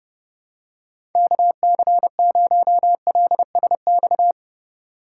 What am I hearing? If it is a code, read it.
Morse code, heard as a keyed tone, about 31 wpm, 700 Hz. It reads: KC0LSX
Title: USA Call Signs - 6 Characters - 2x Character Spacing - 30wpm